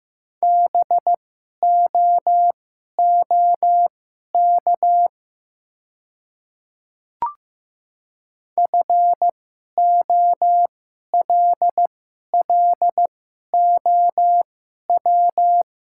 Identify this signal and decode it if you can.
Morse code, heard as a keyed tone, about 15 wpm, spaced as for 11 wpm, 700 Hz. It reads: BOOK FOLLOW
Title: 2x Character Spacing - Top 500 Words - 15wpm